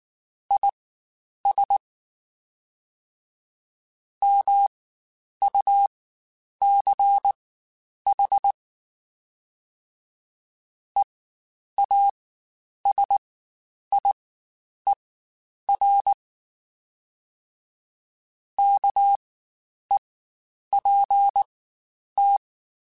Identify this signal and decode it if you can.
Morse code, heard as a keyed tone, about 19 wpm, spaced as for 9 wpm, 800 Hz. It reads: IS MUCH EASIER KEPT